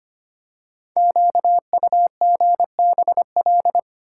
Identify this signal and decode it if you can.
Morse code, heard as a keyed tone, about 25 wpm, 700 Hz. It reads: QUGBL
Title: Code Groups - 5 Letters - 25wpm